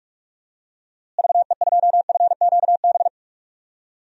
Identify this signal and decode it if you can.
Morse code, heard as a keyed tone, about 45 wpm, 700 Hz. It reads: VE2FQB
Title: DXCC-100 Call Signs - 6 Characters - Repeated 2x - 45wpm